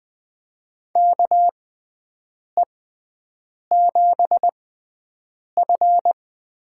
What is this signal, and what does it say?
Morse code, heard as a keyed tone, about 20 wpm, spaced as for 7 wpm, 700 Hz. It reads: KE7F
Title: USA Call Signs - 6 Characters - 6x Character Spacing - 20wpm